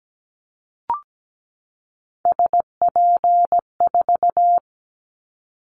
Morse code, heard as a keyed tone, about 17 wpm, 700 Hz. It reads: SP4
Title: Callsign Prefixes for Top 25 DXCC - 17wpm